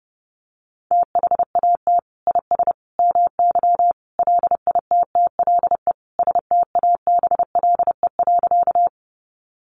Morse code, heard as a keyed tone, about 28 wpm, 700 Hz. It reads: THAT IS MY LITTLE STABLE.
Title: Sentences from Top 1500 Words 30wpm